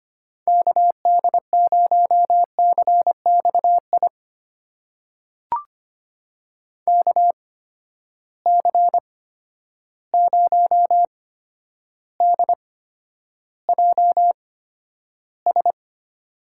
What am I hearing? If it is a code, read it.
Morse code, heard as a keyed tone, about 25 wpm, spaced as for 11 wpm, 700 Hz. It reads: KD0CXI KC0DJS
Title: USA Call Signs - 6 Characters - 8x Character Spacing - 25wpm